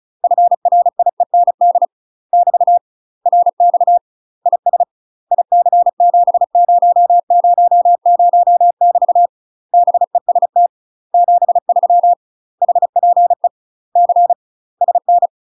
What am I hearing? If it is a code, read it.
Morse code, heard as a keyed tone, about 36 wpm, 700 Hz. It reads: FRIEND <BT> RX IS IC7000 <BT> BEST 73 HPE C SN
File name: UK MASTER QSOs TRAINING / 35WPM / UK MASTER QSOs 0038